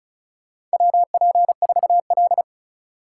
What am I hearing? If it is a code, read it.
Morse code, heard as a keyed tone, about 36 wpm, 700 Hz. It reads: WP4L
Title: USA Call Signs - 4 Characters - 3x Character Spacing - 35wpm